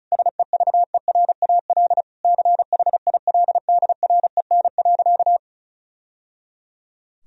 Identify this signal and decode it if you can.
Morse code, heard as a keyed tone, about 34 wpm, 700 Hz. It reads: SEVERAL CHILDREN.
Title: Sentences from Top 1000 Words - Review Part 2 - 35wpm